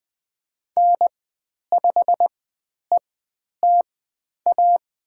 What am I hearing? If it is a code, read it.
Morse code, heard as a keyed tone, about 20 wpm, spaced as for 10 wpm, 700 Hz. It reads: N5ETA
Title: 102 Lesson 17g - Letter u - Call Signs 20wpm